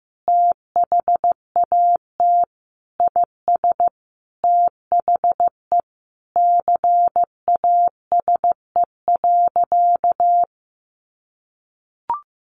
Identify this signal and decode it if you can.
Morse code, heard as a keyed tone, about 15 wpm, 700 Hz. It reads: THAT IS THE CASE.
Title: Sentences from Top 1200 Words 15wpm